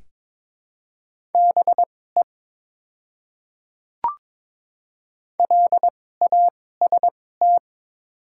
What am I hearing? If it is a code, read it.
Morse code, heard as a keyed tone, about 23 wpm, spaced as for 16 wpm, 700 Hz. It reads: BE LAST
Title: 2x Character Spacing - Top 300 Words - 22wpm